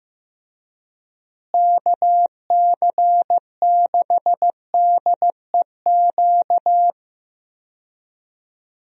Morse code, heard as a keyed tone, about 15 wpm, 700 Hz. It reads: KC6DEQ
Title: USA Call Signs - 6 Characters - Repeated 2x - 15wpm